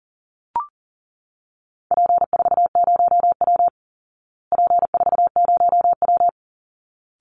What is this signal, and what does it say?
Morse code, heard as a keyed tone, about 39 wpm, 700 Hz. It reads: P40W P40W
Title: Call Signs Repeated 3x - 40wpm